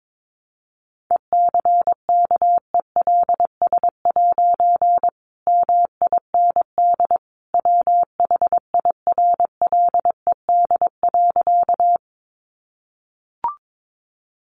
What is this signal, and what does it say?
Morse code, heard as a keyed tone, about 22 wpm, 700 Hz. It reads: ECKELS' MIND WHIRLED.
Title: A Sound Of Thunder 22wpm